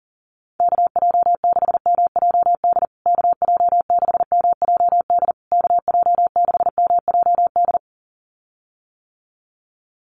Morse code, heard as a keyed tone, about 40 wpm, 700 Hz. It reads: KJ6MJD KJ6MJD KJ6MJD
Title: USA Call Signs Repeated 3x 40wpm